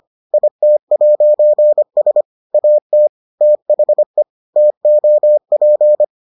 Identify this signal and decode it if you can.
Morse code, heard as a keyed tone, about 25 wpm, 600 Hz. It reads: IT'S AT THE TOP